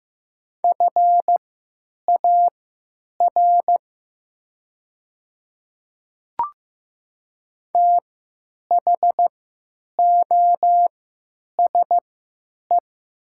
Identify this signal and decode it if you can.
Morse code, heard as a keyed tone, about 15 wpm, spaced as for 9 wpm, 700 Hz. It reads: FAR THOSE
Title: 3x Character Spacing - Top 300 Words - 15wpm